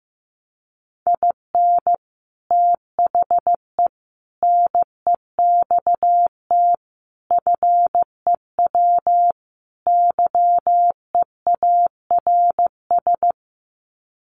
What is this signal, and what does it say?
Morse code, heard as a keyed tone, about 14 wpm, 700 Hz. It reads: IN THE NEXT FEW YEARS
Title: Sets of 5 Words Encore 15wpm